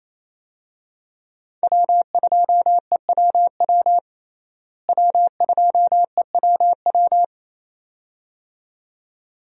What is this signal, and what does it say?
Morse code, heard as a keyed tone, about 27 wpm, 700 Hz. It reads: W2EWW W2EWW
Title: USA Call Signs - 5 Characters - Repeated 2x - 28wpm